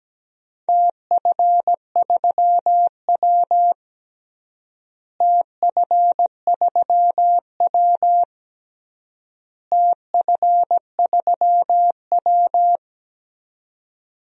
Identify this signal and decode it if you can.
Morse code, heard as a keyed tone, about 17 wpm, 700 Hz. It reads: TF3W TF3W TF3W
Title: DXCC-100 Call Signs - 4 Characters - Repeated 3x - 17wpm